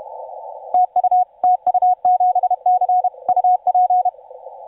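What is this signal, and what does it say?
Morse code, heard as a keyed tone, about 31 wpm, 730 Hz, with radio noise. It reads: TU TU7C UP
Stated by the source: TU7C on 20 metre CW, DX station recording by IZ8GNR.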